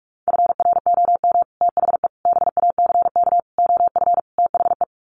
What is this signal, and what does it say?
Morse code, heard as a keyed tone, about 44 wpm, 700 Hz. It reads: FROM THE BACK OF THE
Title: Sets of 5 Words Encore 45wpm